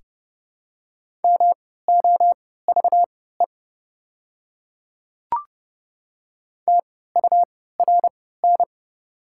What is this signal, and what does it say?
Morse code, heard as a keyed tone, about 30 wpm, spaced as for 17 wpm, 690 Hz. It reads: MOVE TURN
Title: 3x Character Spacing - Top 200 Words - 30wpm